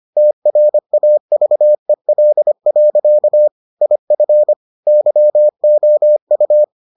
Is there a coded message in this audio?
Yes: Morse code from 0.2 to 6.6 s, heard as a keyed tone, about 25 wpm, 600 Hz. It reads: TRAVEL. IF YOU